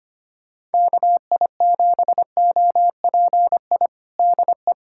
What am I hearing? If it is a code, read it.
Morse code, heard as a keyed tone, about 25 wpm, 700 Hz. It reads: KI7OPI DE
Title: USA Callsign DE Callsign 25wpm